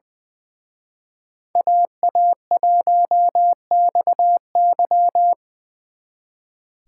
Morse code, heard as a keyed tone, about 24 wpm, 700 Hz. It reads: AA1XY
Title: Speed Racing - USA 2x2 Call Signs 20wpm